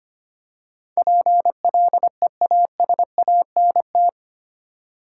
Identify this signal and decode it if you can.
Morse code, heard as a keyed tone, about 25 wpm, 690 Hz. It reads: PLEASANT
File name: 8 Letter Words - First 3 Letters Given - 25wpm